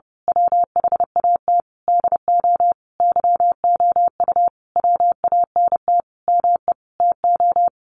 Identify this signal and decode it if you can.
Morse code, heard as a keyed tone, about 30 wpm, 700 Hz. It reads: WHAT DO YOU WANT ME TO
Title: A Sound Of Thunder 30wpm